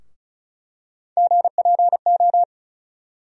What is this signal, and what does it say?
Morse code, heard as a keyed tone, about 34 wpm, 700 Hz. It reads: GPO
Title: USA Call Sign Postfixes - 3 Letters 35wpm